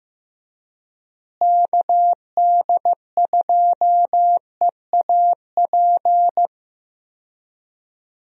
Morse code, heard as a keyed tone, about 16 wpm, 700 Hz. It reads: KD2EAP
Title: USA Call Signs - 6 Characters - Repeated 2x - 15wpm